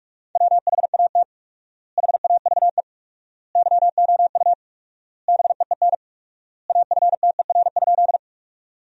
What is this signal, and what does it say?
Morse code, heard as a keyed tone, about 47 wpm, 700 Hz. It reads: WHAT HAVE YOU BEEN AFTER?
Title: 4x Word-Spacing - Sentences from Top 200 Words - 45wpm